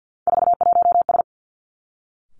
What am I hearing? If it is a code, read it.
Morse code, heard as a keyed tone, about 50 wpm, 700 Hz. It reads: 41S
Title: Three Challenging Letter-Number Combinations 50wpm